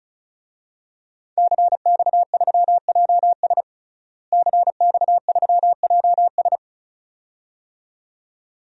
Morse code, heard as a keyed tone, about 34 wpm, 700 Hz. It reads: CX3JS CX3JS
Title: DXCC-100 Call Signs - 5 Characters - Repeated 2x - 35wpm